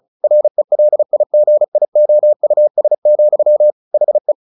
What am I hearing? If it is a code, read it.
Morse code, heard as a keyed tone, about 34 wpm, 600 Hz. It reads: RELIGIOUS, HE
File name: the old man and the sea / the old man and the sea 1 35wpm